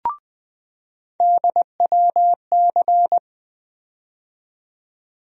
Morse code, heard as a keyed tone, about 20 wpm, 700 Hz. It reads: DWC
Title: USA Call Sign Postfixes - 3 Letters 20wpm